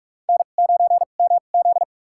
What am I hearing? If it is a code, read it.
Morse code, heard as a keyed tone, about 46 wpm, spaced as for 32 wpm, 700 Hz. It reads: N9MZ
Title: USA Call Signs - 4 Characters - 2x Character Spacing - 45wpm